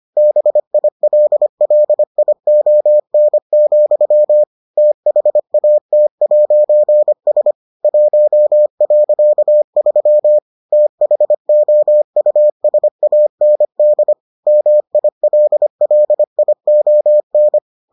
Morse code, heard as a keyed tone, about 25 wpm, 600 Hz. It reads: BILLION, THAT'S 1.3 THOUSAND MILLION